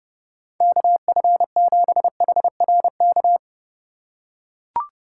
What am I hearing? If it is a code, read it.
Morse code, heard as a keyed tone, about 30 wpm, 700 Hz. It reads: KF7HRK